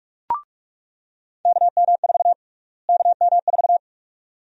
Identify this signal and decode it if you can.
Morse code, heard as a keyed tone, about 46 wpm, 700 Hz. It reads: KM4 KM4
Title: USA Call Sign Prefixes - Repeated 2x - 45wpm